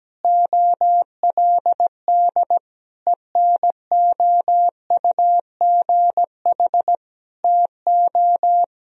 Morse code, heard as a keyed tone, about 17 wpm, 700 Hz. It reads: OLD ENOUGH TO
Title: Sentences from Top 1000 Words - Review Part 1 - 17wpm